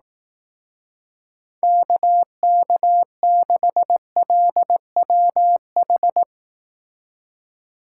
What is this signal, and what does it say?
Morse code, heard as a keyed tone, about 18 wpm, 700 Hz. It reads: KK6LWH
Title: Speed Racing - USA 2x3 Call Signs 15wpm